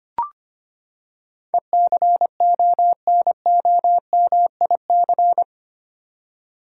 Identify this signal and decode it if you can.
Morse code, heard as a keyed tone, about 25 wpm, 700 Hz. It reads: ECONOMIC
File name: Eight Letter Words - 25wpm